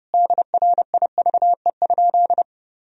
Morse code, heard as a keyed tone, about 30 wpm, 700 Hz. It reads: DRIVE?
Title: Sentences from Top 1000 Words - Review Part 2 - 30wpm